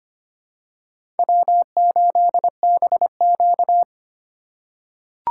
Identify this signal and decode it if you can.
Morse code, heard as a keyed tone, about 25 wpm, 700 Hz. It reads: W8BQ
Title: USA Call Signs - 4 Characters - 2x Character Spacing - 25wpm